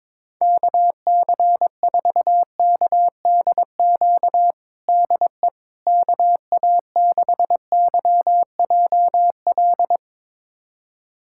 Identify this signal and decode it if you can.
Morse code, heard as a keyed tone, about 21 wpm, 700 Hz. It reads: KC4KDQ DE KA6YJL
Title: USA Callsign DE Callsign - 6 Characters - 22wpm